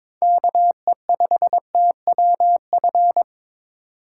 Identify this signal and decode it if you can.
Morse code, heard as a keyed tone, about 22 wpm, 700 Hz. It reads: KE5TWF
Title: USA Call Signs - 6 Characters - 6x Character Spacing - 22wpm